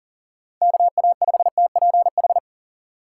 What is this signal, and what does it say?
Morse code, heard as a keyed tone, about 40 wpm, 700 Hz. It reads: KA5TPH
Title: USA Call Signs - 6 Characters - 4x Character Spacing - 40wpm